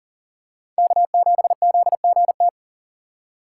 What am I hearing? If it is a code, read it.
Morse code, heard as a keyed tone, about 40 wpm, 700 Hz. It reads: K7ZGT
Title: USA Call Signs - 5 Characters - 3x Character Spacing - 40wpm